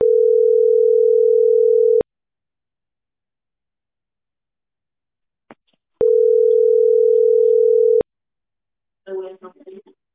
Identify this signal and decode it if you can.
Ringback tone (North America), 440 + 480 Hz.